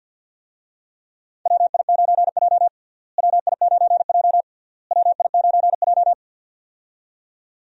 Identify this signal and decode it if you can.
Morse code, heard as a keyed tone, about 52 wpm, 700 Hz. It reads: WI9J WI9J WI9J
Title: USA Call Signs - 4 Characters - Repeated 3x - 50wpm